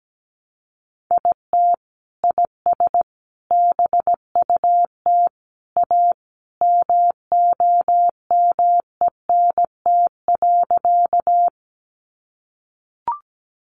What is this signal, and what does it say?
Morse code, heard as a keyed tone, about 17 wpm, 700 Hz. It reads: IT IS BUT A MOMENT.
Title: Sentences from Top 1000 Words 17wpm